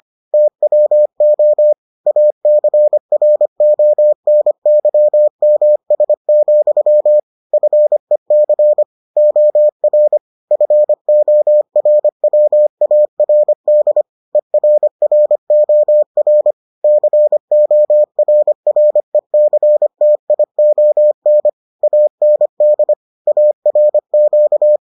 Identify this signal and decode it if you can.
Morse code, heard as a keyed tone, about 25 wpm, 600 Hz. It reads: TWO ACRONYMS, FEC OR FORWARD ERROR CORRECTION AND ARQ